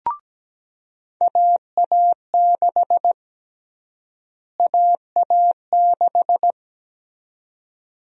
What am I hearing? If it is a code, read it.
Morse code, heard as a keyed tone, about 17 wpm, 700 Hz. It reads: AA6 AA6
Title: USA Call Sign Prefixes - Repeated 2x - 17wpm